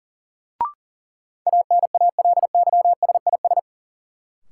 Morse code, heard as a keyed tone, about 40 wpm, 700 Hz. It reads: ANALYSIS
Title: Sentences from Top 1400 Words - New Word Drill Alt - 40wpm